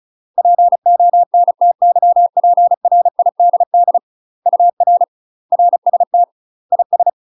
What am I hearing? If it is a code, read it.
Morse code, heard as a keyed tone, about 36 wpm, 700 Hz. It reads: PONTYPRIDD UR RST IS
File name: UK MASTER QSOs TRAINING / 35WPM / UK MASTER QSOs 0013